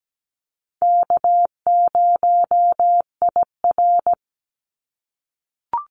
Morse code, heard as a keyed tone, about 17 wpm, 700 Hz. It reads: K0IR
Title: Easy Call Signs - 17wpm